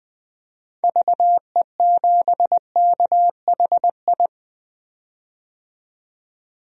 Morse code, heard as a keyed tone, about 21 wpm, 700 Hz. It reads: VE7KHI
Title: DXCC-100 Call Signs - 6 Characters - 20wpm